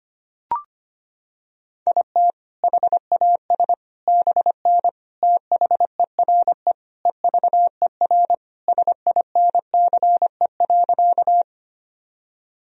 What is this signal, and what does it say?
Morse code, heard as a keyed tone, about 25 wpm, 700 Hz. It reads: IT HAS BN THERE EVER SINCE.
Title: Sentences from Top 400 Words and Abbreviations 25wpm